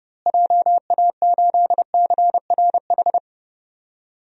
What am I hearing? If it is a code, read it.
Morse code, heard as a keyed tone, about 30 wpm, 700 Hz. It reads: JA8CRH
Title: DXCC-100 Call Signs - 6 Characters - Repeated 3x - 30wpm